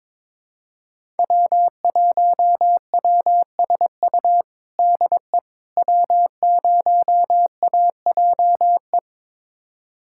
Morse code, heard as a keyed tone, about 22 wpm, 700 Hz. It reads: W1WSU DE W0AJE
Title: USA Callsign DE Callsign - 5 Characters - 22wpm